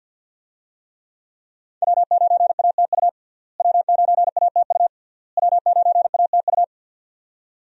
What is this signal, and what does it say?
Morse code, heard as a keyed tone, about 53 wpm, 700 Hz. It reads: W9ATU W9ATU W9ATU
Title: USA Call Signs - 5 Characters - Repeated 3x - 50wpm